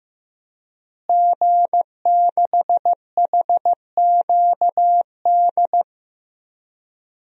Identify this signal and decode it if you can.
Morse code, heard as a keyed tone, about 15 wpm, 700 Hz. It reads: G6HQD
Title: DXCC-100 Call Signs - 5 Characters - Repeated 3x - 15wpm